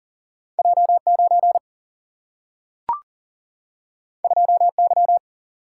Morse code, heard as a keyed tone, about 39 wpm, 700 Hz. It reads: J9 2Y
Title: Single-Letter Single-Number 40wpm